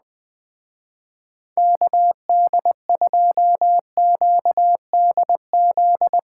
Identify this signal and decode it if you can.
Morse code, heard as a keyed tone, about 20 wpm, 700 Hz. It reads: KD2QDZ
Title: Speed Racing - USA 2x3 Call Signs 17wpm